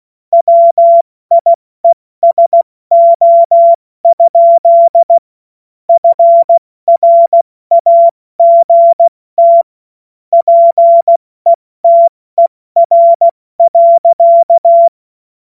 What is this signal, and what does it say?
Morse code, heard as a keyed tone, about 16 wpm, 680 Hz. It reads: WIESO? FRAGT PETER.